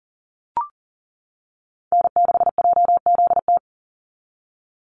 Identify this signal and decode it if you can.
Morse code, heard as a keyed tone, about 38 wpm, 700 Hz. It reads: N6JZT
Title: NA SOTA Activator Callsigns 40wpm